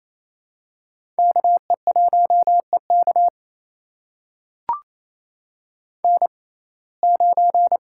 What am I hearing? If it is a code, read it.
Morse code, heard as a keyed tone, about 27 wpm, spaced as for 20 wpm, 700 Hz. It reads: KE1EK N9
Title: USA Call Signs - 5 Characters - 6x Character Spacing - 28wpm